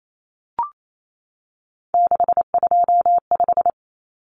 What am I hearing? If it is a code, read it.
Morse code, heard as a keyed tone, about 28 wpm, 700 Hz. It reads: 625
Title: Numbers - 3-Digit 28wpm